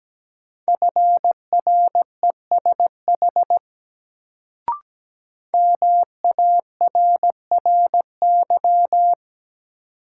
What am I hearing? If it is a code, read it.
Morse code, heard as a keyed tone, about 17 wpm, 700 Hz. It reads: FRESH MARRY
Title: Sentences from Top 1200 Words - New Word Drill Alt - 17wpm